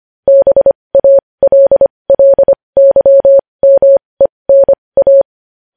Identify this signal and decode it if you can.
Morse code, heard as a keyed tone, about 25 wpm, spaced as for 20 wpm, 550 Hz. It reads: BALLYMENA
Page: news-headlines-in-morse-code-at-20-wpm